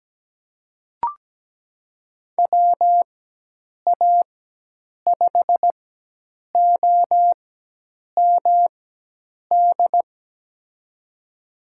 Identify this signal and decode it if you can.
Morse code, heard as a keyed tone, about 17 wpm, spaced as for 8 wpm, 700 Hz. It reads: WA5OMD
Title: USA Call Signs - 6 Characters - 4x Character Spacing - 17wpm